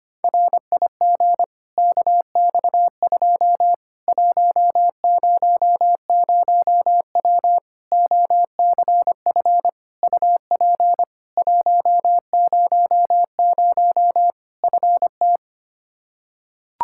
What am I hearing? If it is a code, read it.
Morse code, heard as a keyed tone, about 25 wpm, 700 Hz. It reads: RIG KX2 100W OCF UP 100 FT
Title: Practice QSO - Exchange 2 - Info - 25wpm